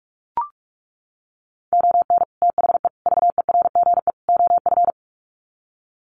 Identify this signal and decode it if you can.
Morse code, heard as a keyed tone, about 44 wpm, 700 Hz. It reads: ON THE VERGE OF
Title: Sets of 4 Words 45wpm